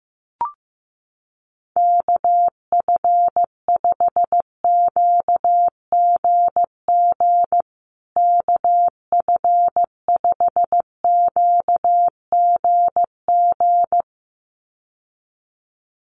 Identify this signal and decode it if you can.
Morse code, heard as a keyed tone, about 15 wpm, 700 Hz. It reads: KF5QGG KF5QGG
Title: USA Call Signs Repeated 2x 15wpm